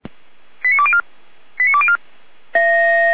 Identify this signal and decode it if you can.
Five-tone selective call (ZVEI), heard as quick runs of five tones, tones of about 70 ms. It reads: 89204 89204
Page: SDR/ZVEI